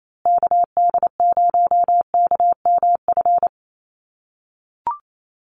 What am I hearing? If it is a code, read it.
Morse code, heard as a keyed tone, about 28 wpm, 700 Hz. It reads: KD0KMF
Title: USA Call Signs Repeated 2x 28wpm